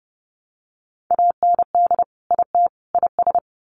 Morse code, heard as a keyed tone, about 30 wpm, 700 Hz. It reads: AND IT IS
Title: Sets of 3 Words 30wpm